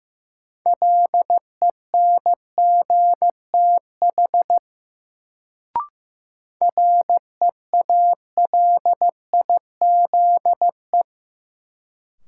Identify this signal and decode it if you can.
Morse code, heard as a keyed tone, about 16 wpm, 700 Hz. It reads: LENGTH REALIZE
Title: Sentences from Top 1000 Words - New Word Drill - 15wpm